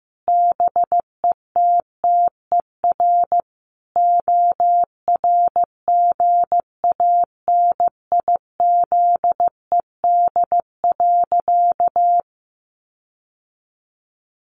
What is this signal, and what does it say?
Morse code, heard as a keyed tone, about 15 wpm, 700 Hz. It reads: BETTER ORGANIZED.
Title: Sentences from Top 1300 Words 15wpm